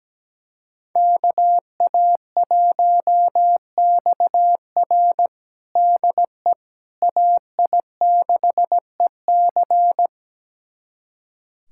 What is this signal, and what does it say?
Morse code, heard as a keyed tone, about 17 wpm, 700 Hz. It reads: KA1XR DE AI6EC
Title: USA Callsign DE Callsign - 5 Characters - 17wpm